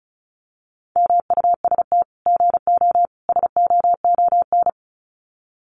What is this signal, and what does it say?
Morse code, heard as a keyed tone, about 34 wpm, 700 Hz. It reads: MUST GO SOON
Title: Common QSO Phrases 35wpm